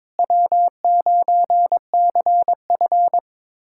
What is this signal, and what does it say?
Morse code, heard as a keyed tone, about 22 wpm, 700 Hz. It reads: W9CF
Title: USA Callsign DE Callsign - 4 Characters - 22wpm